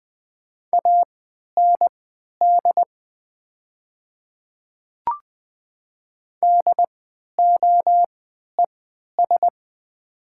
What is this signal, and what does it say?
Morse code, heard as a keyed tone, about 20 wpm, spaced as for 11 wpm, 700 Hz. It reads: AND DOES
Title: 3x Character Spacing - Top 200 Words - 20wpm